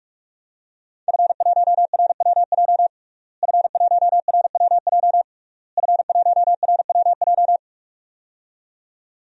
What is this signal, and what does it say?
Morse code, heard as a keyed tone, about 46 wpm, 700 Hz. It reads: F1RWJ F1RWJ F1RWJ
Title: DXCC-100 Call Signs - 5 Characters - Repeated 3x - 45wpm